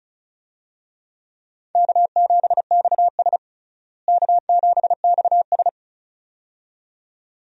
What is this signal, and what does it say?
Morse code, heard as a keyed tone, about 36 wpm, 700 Hz. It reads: K7XS K7XS
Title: USA Call Signs - 4 Characters - Repeated 2x - 35wpm